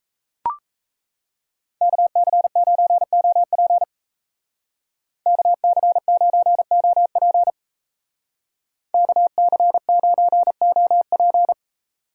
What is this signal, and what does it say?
Morse code, heard as a keyed tone, about 33 wpm, 700 Hz. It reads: KC9OP KC9OP KC9OP
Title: Speed Racing - USA 2x2 Call Signs 28wpm